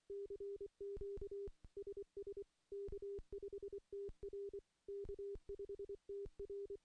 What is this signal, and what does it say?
Morse code, heard as a keyed tone, about 24 wpm, 400 Hz. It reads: CQ SS K5TR K5TR
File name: contests / CW-SS-k5tr- -0800